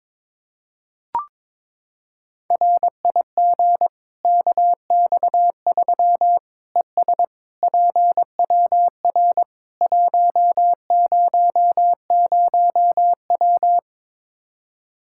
Morse code, heard as a keyed tone, about 21 wpm, 700 Hz. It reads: RIG KX3 ES PWR 100W
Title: Practice QSO - Exchange 2 - Info - 22wpm